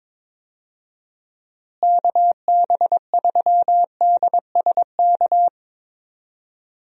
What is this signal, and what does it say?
Morse code, heard as a keyed tone, about 22 wpm, 700 Hz. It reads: KB3DSK
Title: USA Call Signs - 6 Characters - Repeated 2x - 22wpm